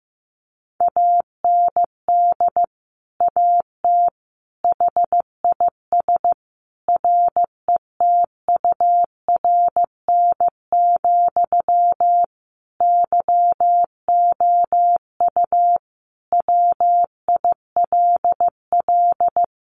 Morse code, heard as a keyed tone, about 15 wpm, 700 Hz. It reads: AND AT HIS RETURN, YOU WILL
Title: Sentences from Top 1500 Words 15wpm